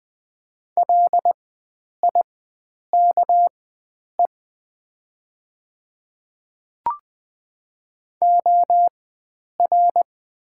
Morse code, heard as a keyed tone, about 20 wpm, spaced as for 9 wpm, 700 Hz. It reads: LIKE OR
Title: 4x Character Spacing - Top 100 Words - 20wpm